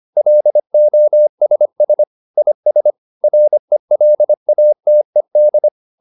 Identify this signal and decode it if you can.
Morse code, heard as a keyed tone, about 25 wpm, 600 Hz. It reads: LOSS IS RELATED
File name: SS211 25